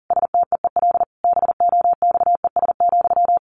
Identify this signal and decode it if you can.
Morse code, heard as a keyed tone, about 40 wpm, 700 Hz. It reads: STEEL BOXES,
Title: A Sound Of Thunder 40wpm